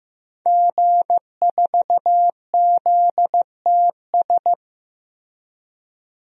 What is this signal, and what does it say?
Morse code, heard as a keyed tone, about 15 wpm, 700 Hz. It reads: G4ZTS
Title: DXCC-100 Call Signs - 5 Characters - Repeated 3x - 15wpm